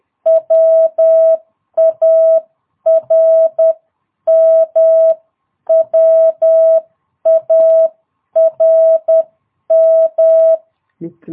Morse code, heard as a keyed tone, about 10 wpm, 660 Hz. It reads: WARM WARM
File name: jh8cbh / 2022cw